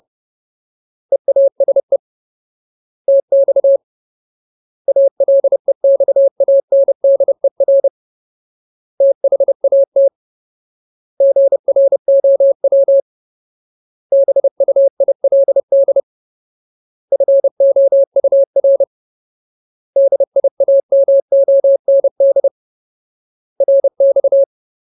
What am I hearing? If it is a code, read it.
Morse code, heard as a keyed tone, about 30 wpm, 550 Hz. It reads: EASE TX ALEXANDER THAT GROW BUILD FOUR DIAMOND RX
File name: Ultimate Word List / 30WPM Ultimate